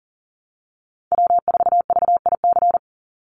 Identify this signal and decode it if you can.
Morse code, heard as a keyed tone, about 40 wpm, 700 Hz. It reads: W4VIC
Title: Call Signs Repeated 3x - 40wpm